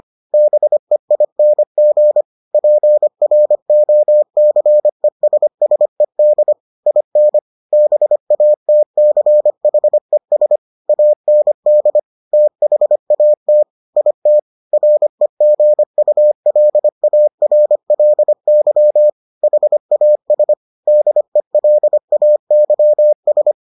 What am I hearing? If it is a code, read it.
Morse code, heard as a keyed tone, about 25 wpm, 600 Hz. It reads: BEING PROCESSED IN BATCHES AND THAT IT REGULARLY HAS DELAYS